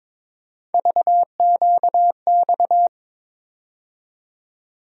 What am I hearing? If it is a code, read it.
Morse code, heard as a keyed tone, about 22 wpm, 700 Hz. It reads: VQX
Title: USA Call Sign Postfixes - 3 Letters 22wpm